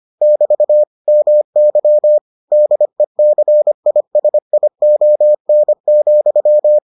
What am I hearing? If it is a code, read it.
Morse code, heard as a keyed tone, about 25 wpm, 600 Hz. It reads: <BT> MY DECISION,